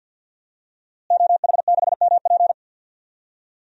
Morse code, heard as a keyed tone, about 50 wpm, 700 Hz. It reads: OH6MP
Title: DXCC-100 Call Signs - 5 Characters - Repeated 3x - 50wpm